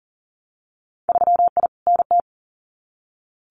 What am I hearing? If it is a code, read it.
Morse code, heard as a keyed tone, about 40 wpm, 700 Hz. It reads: 3I NT
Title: Winter Field Day Exchanges Encore 40wpm